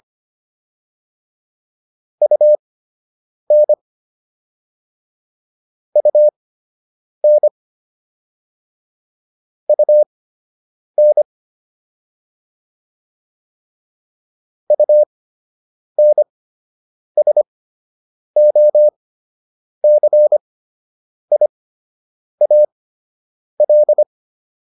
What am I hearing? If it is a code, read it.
Morse code, heard as a keyed tone, about 25 wpm, spaced as for 8 wpm, 600 Hz. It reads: UN UN UN UNSOCIAL